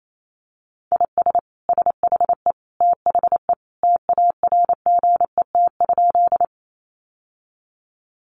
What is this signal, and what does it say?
Morse code, heard as a keyed tone, about 28 wpm, 700 Hz. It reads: IS SHE THE TARGET?
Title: Sentences from Top 1200 Words 28wpm